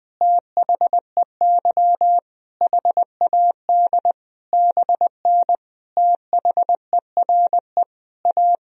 Morse code, heard as a keyed tone, about 20 wpm, 700 Hz. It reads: THEY HAD BN THERE A
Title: Sentences from Top 400 Words and Abbreviations 20wpm